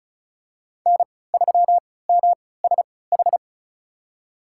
Morse code, heard as a keyed tone, about 36 wpm, spaced as for 20 wpm, 700 Hz. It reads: N3MSH
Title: USA Call Signs - 5 Characters - 3x Character Spacing - 35wpm